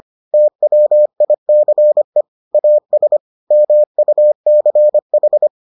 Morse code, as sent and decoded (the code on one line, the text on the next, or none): - .-- .. -.-. / .- ... / -- ..- -.-. ....
TWICE AS MUCH